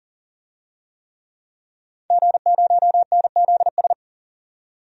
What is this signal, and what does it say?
Morse code, heard as a keyed tone, about 40 wpm, 700 Hz. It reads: G0NZS
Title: DXCC-100 Call Signs - 5 Characters - 40wpm